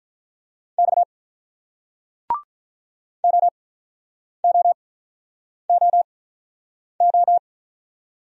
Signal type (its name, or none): Morse code